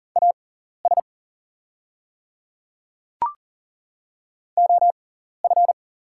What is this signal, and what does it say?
Morse code, heard as a keyed tone, about 41 wpm, spaced as for 14 wpm, 690 Hz. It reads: AS OF
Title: 6x Character Spacing - Top 100 Words - 40wpm